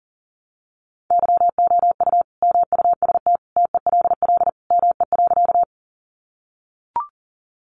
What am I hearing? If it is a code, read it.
Morse code, heard as a keyed tone, about 38 wpm, 700 Hz. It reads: YOU MUST TELL ME.
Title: Sentences from Top 200 Words 40wpm